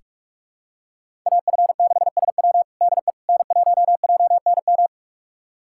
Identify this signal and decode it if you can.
Morse code, heard as a keyed tone, about 42 wpm, 700 Hz. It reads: AF6SW DE N1JNM
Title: USA Callsign DE Callsign - 5 Characters - 45wpm